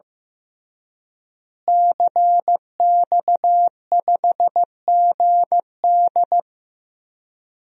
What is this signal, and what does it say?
Morse code, heard as a keyed tone, about 15 wpm, 700 Hz. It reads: CX5GD
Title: DXCC-100 Call Signs - 5 Characters - Repeated 2x - 15wpm